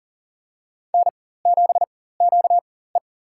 Morse code, heard as a keyed tone, about 40 wpm, spaced as for 19 wpm, 700 Hz. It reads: N7QE
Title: USA Call Signs - 4 Characters - 4x Character Spacing - 40wpm